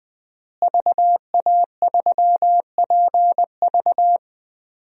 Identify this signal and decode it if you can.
Morse code, heard as a keyed tone, about 20 wpm, 700 Hz. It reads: VA3PV
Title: DXCC-100 Call Signs - 5 Characters - Repeated 3x - 20wpm